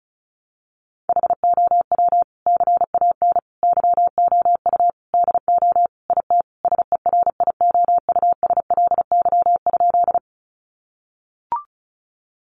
Morse code, heard as a keyed tone, about 34 wpm, 700 Hz. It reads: HOW CAN YOU DO IT SERIOUSLY?
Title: Sentences from Top 1200 Words 35wpm